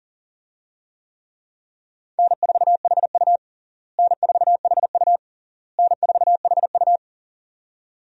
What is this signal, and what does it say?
Morse code, heard as a keyed tone, about 37 wpm, 700 Hz. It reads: N4HU N4HU N4HU
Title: USA Call Signs - 4 Characters - Repeated 3x - 40wpm